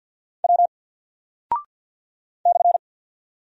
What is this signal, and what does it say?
Morse code, heard as a keyed tone, about 48 wpm, 700 Hz. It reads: W /
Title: Single Character (Letter:Number:Punctuation:Prosign) 50wpm